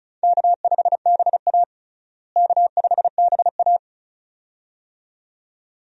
Morse code, heard as a keyed tone, about 34 wpm, 700 Hz. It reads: K5BA K5BA
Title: USA Call Signs - 4 Characters - Repeated 2x - 35wpm